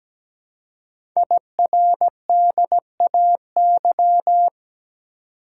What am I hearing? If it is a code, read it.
Morse code, heard as a keyed tone, about 17 wpm, 700 Hz. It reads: IRDAY
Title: Code Groups - 5 Letters - 17wpm